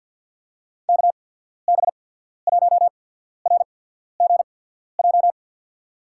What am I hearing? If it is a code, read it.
Morse code, heard as a keyed tone, about 52 wpm, spaced as for 14 wpm, 700 Hz. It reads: KB1RGJ
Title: USA Call Signs - 6 Characters - 8x Character Spacing - 50wpm